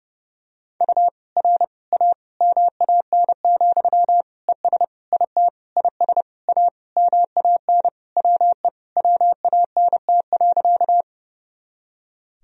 Morse code, heard as a keyed tone, about 30 wpm, 700 Hz. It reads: U R A MAN, ES IT IS A MAN WE WANT.
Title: Sentences from Top 200 Words and Abbreviations 30wpm